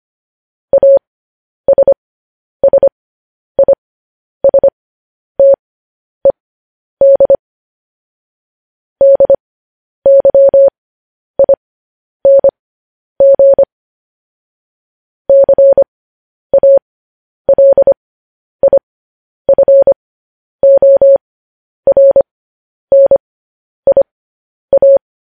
Morse code, heard as a keyed tone, about 25 wpm, spaced as for 10 wpm, 550 Hz. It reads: ASSISTED DYING CALIFORNIA
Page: News Headlines in Morse Code at 10 WPM